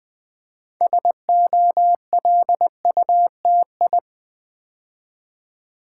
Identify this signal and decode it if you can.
Morse code, heard as a keyed tone, about 20 wpm, 700 Hz. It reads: SOLUTI
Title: Build A Word - 2 Letters At A Time - 8-Letter Words - 20wpm